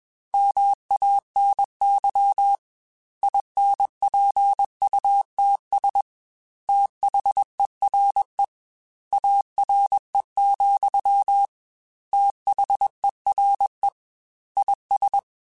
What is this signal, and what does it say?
Morse code, heard as a keyed tone, about 21 wpm, 800 Hz. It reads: MANY INPUTS THERE ARE, THERE IS
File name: Technical Subjects / Technical Subjects-mp3s / AND Gate @20